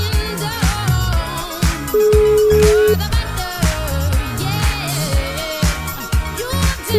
Ringback tone (Europe, 425 Hz), with noise.